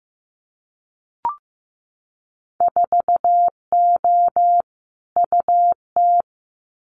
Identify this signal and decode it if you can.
Morse code, heard as a keyed tone, about 15 wpm, 700 Hz. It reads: 4O UT
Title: Winter Field Day Exchanges Encore 15wpm